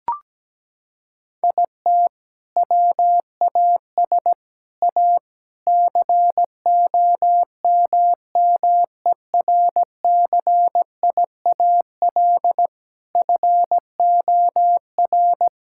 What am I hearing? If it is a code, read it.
Morse code, heard as a keyed tone, about 17 wpm, 700 Hz. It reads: IT WAS A COMMERCIAL FOR
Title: Sentences from Top 1000 Words - Review Part 5 - 17wpm